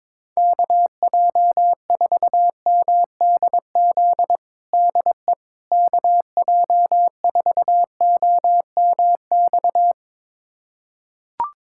Morse code, heard as a keyed tone, about 22 wpm, 700 Hz. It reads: KJ4MDZ DE KJ4OMX
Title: USA Callsign DE Callsign - 6 Characters - 22wpm